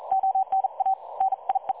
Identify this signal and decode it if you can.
Morse code, heard as a keyed tone, about 40 wpm, 770 Hz. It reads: ONA NIE
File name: dlazaawansowanych / nagraniazpasma / websdr recording 3565.0kHz